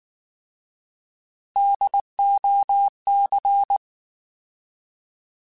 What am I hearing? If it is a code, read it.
Morse code, heard as a keyed tone, about 19 wpm, 800 Hz. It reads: DOC
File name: Alphabetical / Alphabetical mp3s / Ds @18